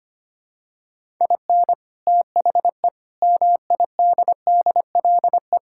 Morse code, heard as a keyed tone, about 25 wpm, 690 Hz. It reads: IN THE MIDDLE
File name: practice / 25wpm / Sentences with 2-Word Phrases - 25wpm